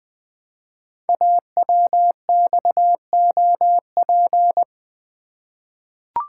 Morse code, heard as a keyed tone, about 20 wpm, 700 Hz. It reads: AWXOP
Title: Code Groups - 5 Letters - 20wpm